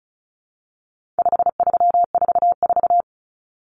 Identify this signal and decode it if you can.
Morse code, heard as a keyed tone, about 34 wpm, 700 Hz. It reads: 5344
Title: Numbers - 4-Digit 35wpm